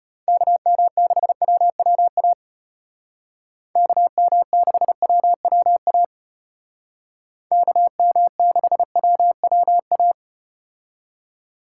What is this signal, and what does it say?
Morse code, heard as a keyed tone, about 34 wpm, 700 Hz. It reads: KM6WWA KM6WWA KM6WWA